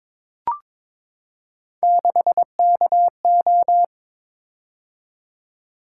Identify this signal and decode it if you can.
Morse code, heard as a keyed tone, about 22 wpm, 700 Hz. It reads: 6KO
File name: practice / 22wpm / Three Letter-Number Combinations - 22wpm